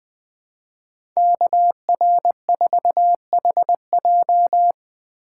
Morse code, heard as a keyed tone, about 20 wpm, 700 Hz. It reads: KR4HJ
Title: USA Call Signs - 5 Characters - 4x Character Spacing - 20wpm